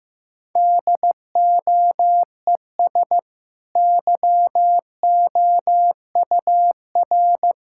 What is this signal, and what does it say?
Morse code, heard as a keyed tone, about 15 wpm, 700 Hz. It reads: DOES YOUR
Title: Sentences from Top 1000 Words - Review Part 6 - 15wpm